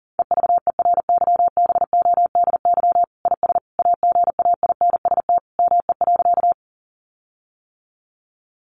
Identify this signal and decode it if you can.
Morse code, heard as a keyed tone, about 40 wpm, 700 Hz. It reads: EVERYBODY IS AGAINST ME.
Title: Sentences from Top 1400 Words 40wpm